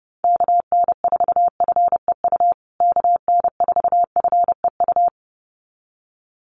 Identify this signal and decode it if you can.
Morse code, heard as a keyed tone, about 30 wpm, 700 Hz. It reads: KN4FEU KN4FEU
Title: USA Call Signs Repeated 2x 30wpm